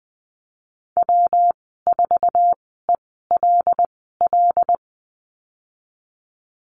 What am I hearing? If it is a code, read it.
Morse code, heard as a keyed tone, about 20 wpm, spaced as for 15 wpm, 700 Hz. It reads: W4ELL
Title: 167 Lesson 27g - Number 7 - Call Signs 20wpm